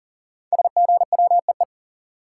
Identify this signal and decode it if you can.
Morse code, heard as a keyed tone, about 39 wpm, 700 Hz. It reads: SGWEE
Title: Code Groups - 5 Letters - 40wpm